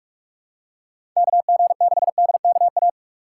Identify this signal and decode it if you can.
Morse code, heard as a keyed tone, about 41 wpm, 700 Hz. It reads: KG6DKA
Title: USA Call Signs - 4x Character Spacing - 45wpm